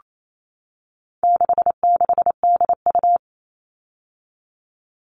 Morse code, heard as a keyed tone, about 28 wpm, 700 Hz. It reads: 66DU
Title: All Cut Numbers - 4-Digit 28wpm